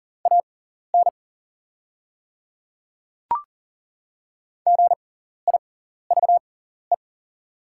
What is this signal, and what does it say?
Morse code, heard as a keyed tone, about 40 wpm, spaced as for 14 wpm, 700 Hz. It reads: AN GIVE
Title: 6x Character Spacing - Top 200 Words - 40wpm